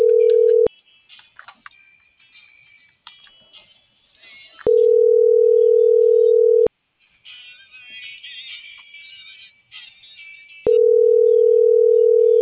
Ringback tone (North America), 440 + 480 Hz.